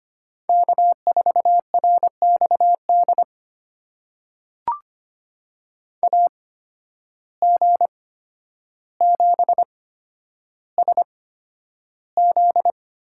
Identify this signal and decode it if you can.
Morse code, heard as a keyed tone, about 25 wpm, spaced as for 11 wpm, 700 Hz. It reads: K4RXD AG7SZ
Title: USA Call Signs - 5 Characters - 8x Character Spacing - 25wpm